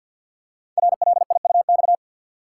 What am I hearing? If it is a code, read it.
Morse code, heard as a keyed tone, about 47 wpm, 700 Hz. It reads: RLIUX
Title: Code Groups - 5 Letter:Numbers - 50wpm